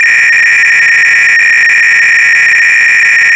300-baud modem data.